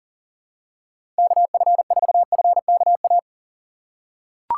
Morse code, heard as a keyed tone, about 40 wpm, 700 Hz. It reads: KF4FKA
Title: USA Call Signs - 6 Characters - 2x Character Spacing - 40wpm